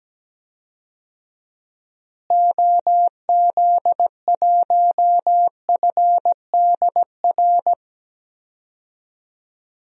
Morse code, heard as a keyed tone, about 17 wpm, 700 Hz. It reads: OZ1FDR
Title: DXCC-100 Call Signs - 6 Characters - Repeated 2x - 17wpm